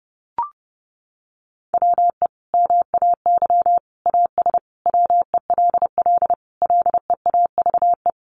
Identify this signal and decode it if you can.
Morse code, heard as a keyed tone, about 30 wpm, 700 Hz. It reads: WE MAY AS WELL LEAVE
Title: Sentences from Top 800 Words 30wpm